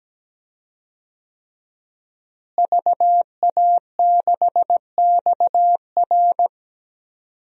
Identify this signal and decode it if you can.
Morse code, heard as a keyed tone, about 17 wpm, 700 Hz. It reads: VA6XR
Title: DXCC-100 Call Signs - 5 Characters - Repeated 3x - 17wpm